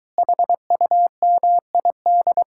Morse code, heard as a keyed tone, about 23 wpm, 700 Hz. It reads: HUMID